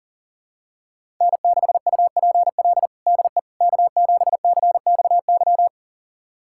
Morse code, heard as a keyed tone, about 40 wpm, 700 Hz. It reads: N6UPL DE K7CXY
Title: USA Callsign DE Callsign - 5 Characters - 40wpm